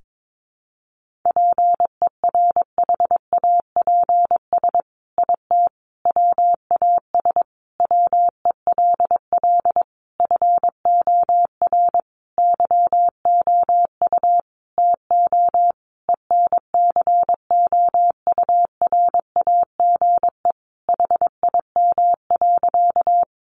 Morse code, heard as a keyed tone, about 22 wpm, 700 Hz. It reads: PERHAPS IT WAS WELL FOR YOU TO ENCOURAGE HIM.
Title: Sentences from Top 1400 Words 22wpm